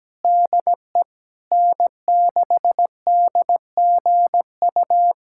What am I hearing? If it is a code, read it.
Morse code, heard as a keyed tone, about 17 wpm, 700 Hz. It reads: DE N6DGU
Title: USA Callsign DE Callsign - 5 Characters - 17wpm